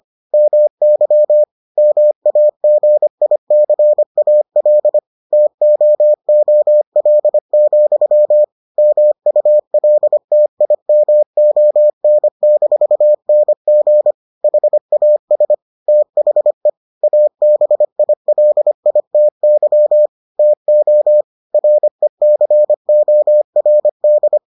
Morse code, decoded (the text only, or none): MY MAGICAL TOOL, MULTIMON-NG HAS THE ABILITY TO RECORD